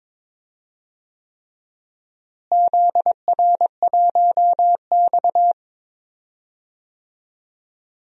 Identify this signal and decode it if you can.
Morse code, heard as a keyed tone, about 22 wpm, 700 Hz. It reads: ZR1X
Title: DXCC-100 Call Signs - 4 Characters - Repeated 3x - 22wpm